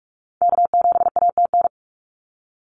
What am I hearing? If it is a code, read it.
Morse code, heard as a keyed tone, about 44 wpm, 700 Hz. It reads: K7ATN
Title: NA SOTA Activator Callsigns 45wpm